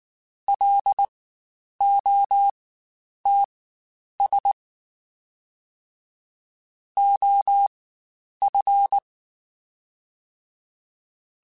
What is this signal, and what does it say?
Morse code, heard as a keyed tone, about 19 wpm, spaced as for 9 wpm, 800 Hz. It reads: LOTS OF
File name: Spaced Characters / Spaced Characters-mp3s / Weather5 12 sp @18